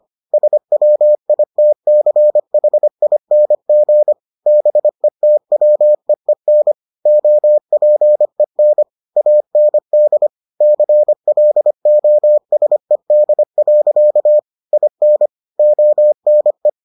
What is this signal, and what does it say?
Morse code, heard as a keyed tone, about 25 wpm, 600 Hz. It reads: SWITCHING BETWEEN OPEN AND CLOSED. IN ONE